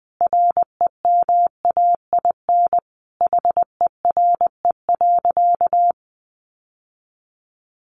Morse code, heard as a keyed tone, about 20 wpm, 700 Hz. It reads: REMAIN HERE.
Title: Sentences from Top 1200 Words 20wpm